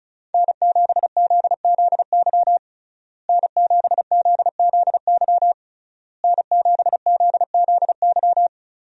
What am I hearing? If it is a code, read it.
Morse code, heard as a keyed tone, about 35 wpm, 700 Hz. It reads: N7ZZY N7ZZY N7ZZY
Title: USA Call Signs - 5 Characters - Repeated 3x - 35wpm